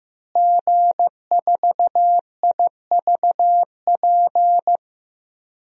Morse code, heard as a keyed tone, about 15 wpm, 700 Hz. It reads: G4IVP
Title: DXCC-100 Call Signs - 5 Characters - Repeated 3x - 15wpm